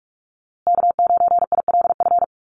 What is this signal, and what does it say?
Morse code, heard as a keyed tone, about 44 wpm, 700 Hz. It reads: K9ILF
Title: USA Call Signs Repeated 3x 45wpm